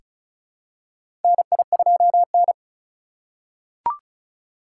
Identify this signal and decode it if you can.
Morse code, heard as a keyed tone, about 36 wpm, 700 Hz. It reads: NI2N
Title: USA Call Signs - 4 Characters - 3x Character Spacing - 35wpm